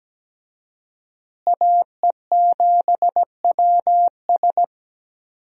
Morse code, heard as a keyed tone, about 17 wpm, 700 Hz. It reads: AE7WS
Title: USA Call Signs - 5 Characters - Repeated 2x - 17wpm